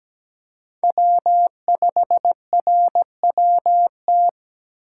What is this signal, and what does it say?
Morse code, heard as a keyed tone, about 17 wpm, 700 Hz. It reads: W5RWT
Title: USA Call Signs - 5 Characters - 2x Character Spacing - 17wpm